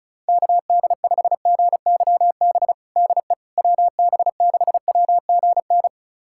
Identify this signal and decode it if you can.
Morse code, heard as a keyed tone, about 36 wpm, 700 Hz. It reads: KD5GYB DE WB6WGN
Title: USA Callsign DE Callsign - 6 Characters - 35wpm